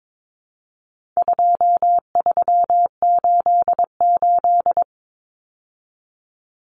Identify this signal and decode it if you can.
Morse code, heard as a keyed tone, about 22 wpm, 700 Hz. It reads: 2388
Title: Numbers - 4-Digit 22wpm